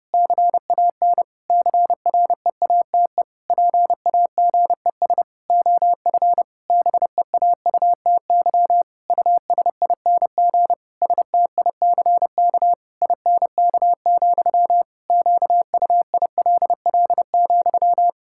Morse code, heard as a keyed tone, about 30 wpm, 700 Hz. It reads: CAN CREATE PAGES OF BEAUTY USING STICK INK, QUILL,
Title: Pangram Challenge v2 - 30wpm